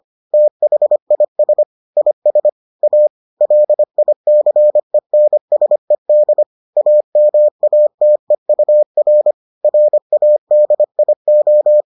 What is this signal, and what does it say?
Morse code, heard as a keyed tone, about 25 wpm, 600 Hz. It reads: THIS IS A LICENSED AMATEUR RADIO